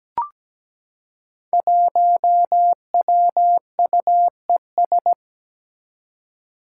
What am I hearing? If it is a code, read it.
Morse code, heard as a keyed tone, about 17 wpm, 700 Hz. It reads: 1WUES
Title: Five Challenging Letter-Number Combinations 17wpm